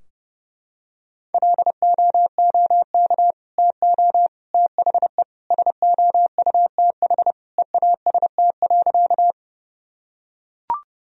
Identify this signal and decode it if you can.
Morse code, heard as a keyed tone, about 30 wpm, 700 Hz. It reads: LOOK TO THE SOUTH EAST.
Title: Sentences from Top 1000 Words - Review Part 3 - 30wpm